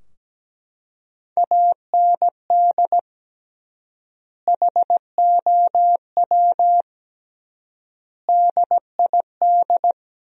Morse code, heard as a keyed tone, about 17 wpm, 700 Hz. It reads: AND HOW DID
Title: 3x Word-Spacing - Sentences from Top 100 Words - 17wpm